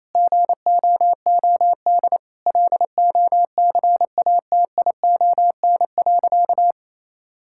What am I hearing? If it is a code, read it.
Morse code, heard as a keyed tone, about 28 wpm, 700 Hz. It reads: GOOD LOCATION.
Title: Sentences from Top 1000 Words - Review Part 6 - 28wpm